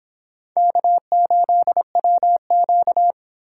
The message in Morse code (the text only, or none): K8WQ